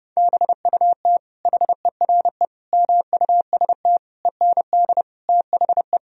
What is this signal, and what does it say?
Morse code, heard as a keyed tone, about 28 wpm, 700 Hz. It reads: BUT HERE MUST END THE
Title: Sentences from Top 1000 Words - Review Part 1 - 30wpm